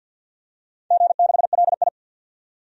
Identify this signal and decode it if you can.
Morse code, heard as a keyed tone, about 48 wpm, 700 Hz. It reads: G6LI
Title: DXCC-100 Call Signs - 4 Characters - 50wpm